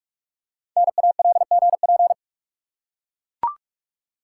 Morse code, heard as a keyed tone, about 44 wpm, 700 Hz. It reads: NALGP
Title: Code Groups - 5 Letters - 45wpm